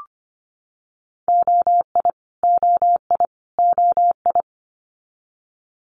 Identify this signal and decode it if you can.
Morse code, heard as a keyed tone, about 25 wpm, 700 Hz. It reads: OI OI OI
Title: Multi-letter Phonemes Repeated 3x 25wpm